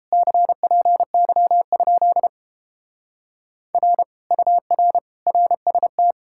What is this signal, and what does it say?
Morse code, heard as a keyed tone, about 33 wpm, 700 Hz. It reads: CPY? R UR RST